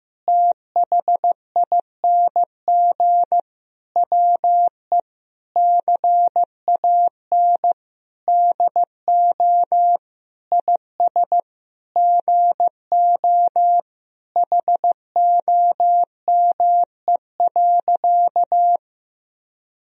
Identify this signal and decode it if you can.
Morse code, heard as a keyed tone, about 15 wpm, 700 Hz. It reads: THING WE CAN DO IS GO HOME.
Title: Sentences from Top 200 Words and Abbreviations 15wpm